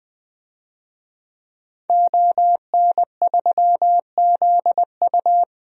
Morse code, heard as a keyed tone, about 20 wpm, 700 Hz. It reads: ON3ZU
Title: DXCC-100 Call Signs - 5 Characters - Repeated 3x - 20wpm